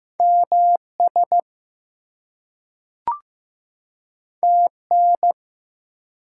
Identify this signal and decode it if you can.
Morse code, heard as a keyed tone, about 15 wpm, 700 Hz. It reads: MS TN
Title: US States and Canadian Provinces 15wpm